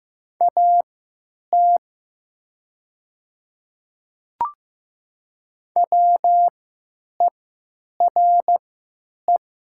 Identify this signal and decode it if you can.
Morse code, heard as a keyed tone, about 15 wpm, spaced as for 9 wpm, 700 Hz. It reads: AT WERE
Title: 3x Character Spacing - Top 100 Words - 15wpm